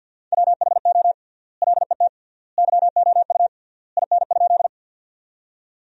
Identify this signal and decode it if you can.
Morse code, heard as a keyed tone, about 50 wpm, 700 Hz. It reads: WHO LET YOU IN?
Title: 3x Word-Spacing - Sentences from Top 300 Words - 50wpm